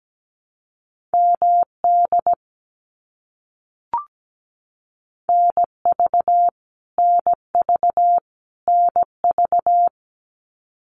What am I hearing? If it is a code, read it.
Morse code, heard as a keyed tone, about 17 wpm, 700 Hz. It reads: MD NV NV NV
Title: US States Repeated 3x 17wpm